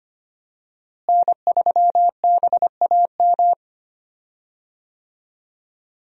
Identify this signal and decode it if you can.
Morse code, heard as a keyed tone, about 25 wpm, 700 Hz. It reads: N3BAM
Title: USA Call Signs - 5 Characters - Repeated 2x - 25wpm